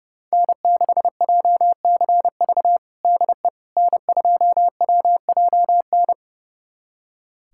Morse code, heard as a keyed tone, about 30 wpm, 700 Hz. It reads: N6JCV DE N2WJN
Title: USA Callsign DE Callsign - 5 Characters - 30wpm